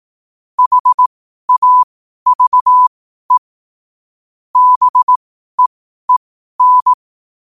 Morse code, heard as a keyed tone, about 18 wpm, spaced as for 12 wpm, 1000 Hz. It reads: HAVE BEEN
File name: qotdfiles / 12wpm